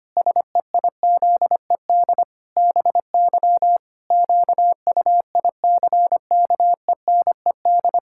Morse code, heard as a keyed tone, about 25 wpm, 700 Hz. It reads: SEIZED BY QUICKENED
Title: Pangram Challenge v2 - 25wpm